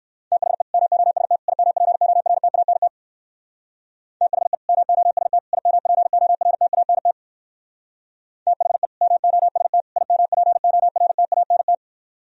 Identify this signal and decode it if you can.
Morse code, heard as a keyed tone, about 58 wpm, 700 Hz. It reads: THE MOST IMPORTANT THE MOST IMPORTANT THE MOST IMPORTANT